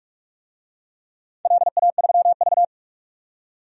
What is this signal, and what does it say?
Morse code, heard as a keyed tone, about 42 wpm, 700 Hz. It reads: LA3V
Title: DXCC-100 Call Signs - 4 Characters - 45wpm